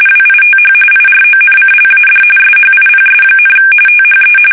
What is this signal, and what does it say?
RTTY, about 45.45 baud. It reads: WITH IRONCLAD SHIPS, THE
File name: fsk850 / usnavy3